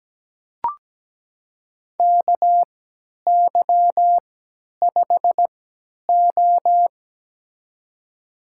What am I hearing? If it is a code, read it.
Morse code, heard as a keyed tone, about 17 wpm, spaced as for 10 wpm, 700 Hz. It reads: KY5O
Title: USA Call Signs - 3x Character Spacing - 17wpm